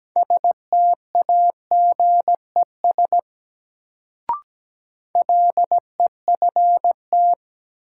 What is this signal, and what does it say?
Morse code, heard as a keyed tone, about 17 wpm, 700 Hz. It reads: STAGES LEFT
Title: Sets of 5 Words Encore - New Word Drill - 17wpm